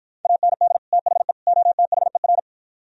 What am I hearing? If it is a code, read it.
Morse code, heard as a keyed tone, about 50 wpm, 700 Hz. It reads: AND THE OTHER